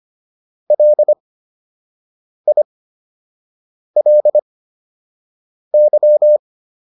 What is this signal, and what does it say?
Morse code, heard as a keyed tone, about 25 wpm, spaced as for 6 wpm, 600 Hz. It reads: LILY